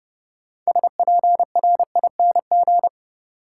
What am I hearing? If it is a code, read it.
Morse code, heard as a keyed tone, about 30 wpm, 700 Hz. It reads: SPRING
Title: Sentences from Top 1200 Words - New Word Drill Alt - 30wpm